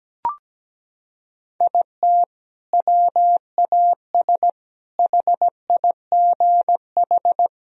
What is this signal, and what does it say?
Morse code, heard as a keyed tone, about 17 wpm, 700 Hz. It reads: IT WAS HIGH
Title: Sentences from Top 1000 Words - Review Part 2 - 17wpm